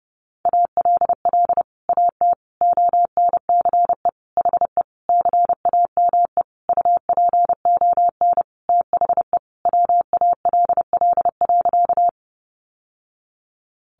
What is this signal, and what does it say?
Morse code, heard as a keyed tone, about 30 wpm, 700 Hz. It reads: ALL AT ONCE HE CAME UPON THE WALL.
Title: Sentences from Top 1100 Words 30wpm